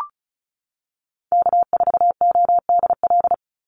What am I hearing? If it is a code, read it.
Morse code, heard as a keyed tone, about 34 wpm, 700 Hz. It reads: K4ODL
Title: Call Signs - 35wpm